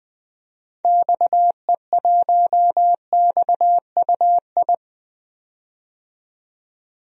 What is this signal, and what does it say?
Morse code, heard as a keyed tone, about 20 wpm, 700 Hz. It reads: XE1XUI
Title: DXCC-100 Call Signs - 6 Characters - 20wpm